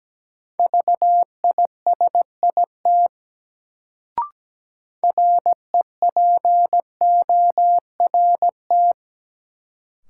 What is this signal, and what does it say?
Morse code, heard as a keyed tone, about 17 wpm, 700 Hz. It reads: VISIT REPORT
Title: Sentences from Top 900 Words - New Word Drill - 17wpm